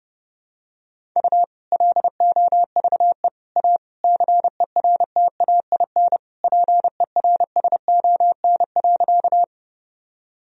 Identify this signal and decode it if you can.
Morse code, heard as a keyed tone, about 30 wpm, 700 Hz. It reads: U LOVE A CERTAIN PERSON.
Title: Sentences from Top 500 Words and Abbreviations 30wpm